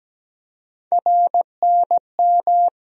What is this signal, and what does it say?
Morse code, heard as a keyed tone, about 17 wpm, 700 Hz. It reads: RNM
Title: USA Call Sign Postfixes - 3 Letters 17wpm